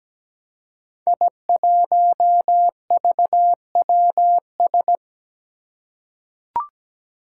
Morse code, heard as a keyed tone, about 17 wpm, 700 Hz. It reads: I1VWS
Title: Five Challenging Letter-Number Combinations 17wpm